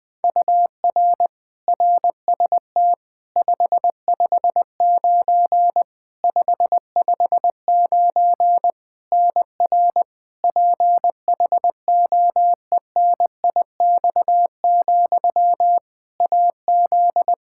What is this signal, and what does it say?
Morse code, heard as a keyed tone, about 21 wpm, 700 Hz. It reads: UR RST 559 559 NR PHOENIX, AZ
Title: Practice QSO - Exchange 1 - Beginning - 20wpm